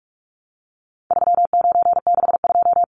Morse code, heard as a keyed tone, about 44 wpm, 700 Hz. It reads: 3962
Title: Numbers - 4-Digit 45wpm